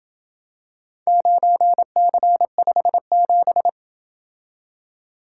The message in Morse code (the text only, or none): 9C57